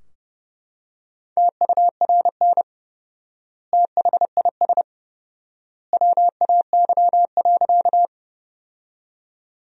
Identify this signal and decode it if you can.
Morse code, heard as a keyed tone, about 30 wpm, 700 Hz. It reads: TURN THIS WAY.
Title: 4x Word-Spacing - Sentences from Top 200 Words - 30wpm